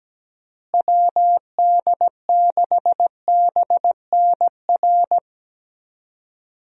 Morse code, heard as a keyed tone, about 17 wpm, 700 Hz. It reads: WD6BNR
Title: USA Call Signs - 6 Characters - Repeated 3x - 17wpm